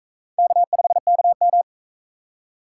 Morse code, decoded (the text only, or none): K5KM